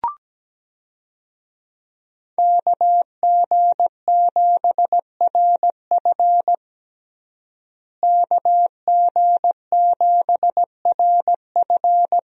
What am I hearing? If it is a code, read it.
Morse code, heard as a keyed tone, about 17 wpm, 700 Hz. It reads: KG7RF KG7RF
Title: USA Call Signs - 5 Characters - Repeated 3x - 17wpm